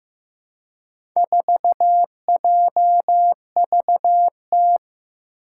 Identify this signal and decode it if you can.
Morse code, heard as a keyed tone, about 15 wpm, 700 Hz. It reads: 4JVT
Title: Four Challenging Letter-Number Combinations 15wpm